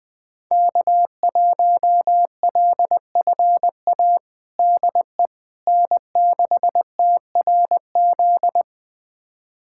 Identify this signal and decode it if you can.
Morse code, heard as a keyed tone, about 20 wpm, 700 Hz. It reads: K1LFA DE N6TRZ
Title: USA Callsign DE Callsign - 5 Characters - 20wpm